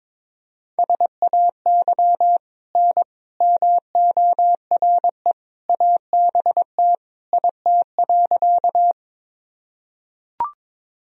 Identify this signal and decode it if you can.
Morse code, heard as a keyed tone, about 21 wpm, 700 Hz. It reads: SAY N MORE ABT IT.
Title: Sentences from Top 200 Words and Abbreviations 22wpm